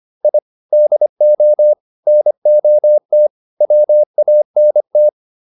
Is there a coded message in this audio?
Yes: Morse code from 0.2 to 5.1 s, heard as a keyed tone, about 25 wpm, 600 Hz. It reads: I DO NOT WANT